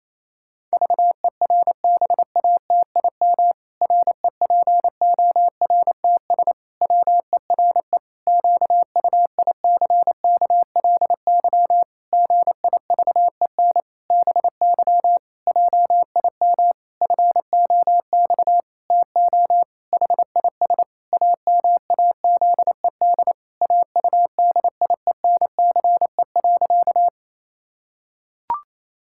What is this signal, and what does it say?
Morse code, heard as a keyed tone, about 28 wpm, 700 Hz. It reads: VERBATIM REPORTS WERE QUICKLY GIVEN BY JIM FOX TO HIS AMAZED AUDIENCE.
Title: Pangram Challenge v2 - 28wpm